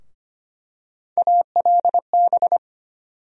Morse code, heard as a keyed tone, about 25 wpm, 700 Hz. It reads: ALB
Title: USA Call Sign Postfixes - 3 Letters 25wpm